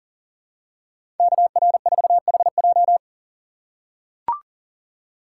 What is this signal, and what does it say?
Morse code, heard as a keyed tone, about 40 wpm, 700 Hz. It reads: KR4HJ
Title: USA Call Signs - 5 Characters - 4x Character Spacing - 40wpm